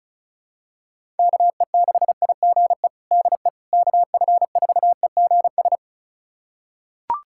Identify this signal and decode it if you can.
Morse code, heard as a keyed tone, about 34 wpm, 700 Hz. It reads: KE6IGE DE KF4EGS
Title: USA Callsign DE Callsign - 6 Characters - 35wpm